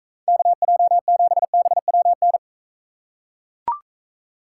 Morse code, heard as a keyed tone, about 42 wpm, 700 Hz. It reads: KJ7BWN